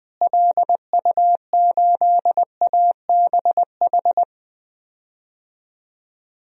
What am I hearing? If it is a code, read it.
Morse code, heard as a keyed tone, about 20 wpm, 700 Hz. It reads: LU8ABH
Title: DXCC-100 Call Signs - 6 Characters - Repeated 2x - 20wpm